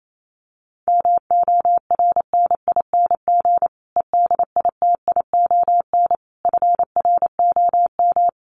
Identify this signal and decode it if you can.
Morse code, heard as a keyed tone, about 28 wpm, 700 Hz. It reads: MORNING EDITION FROM
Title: Sets of 5 Words Encore 28wpm